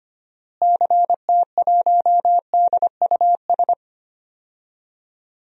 Morse code, heard as a keyed tone, about 25 wpm, 700 Hz. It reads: CT1DUS
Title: DXCC-100 Call Signs - 6 Characters - Repeated 2x - 25wpm